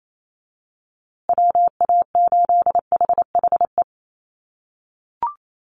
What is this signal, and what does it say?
Morse code, heard as a keyed tone, about 28 wpm, 700 Hz. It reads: WA8HHE
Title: USA Call Signs Repeated 2x 28wpm